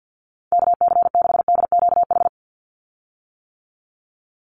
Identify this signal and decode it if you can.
Morse code, heard as a keyed tone, about 48 wpm, 700 Hz. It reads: KC6DQH
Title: USA Call Signs 50wpm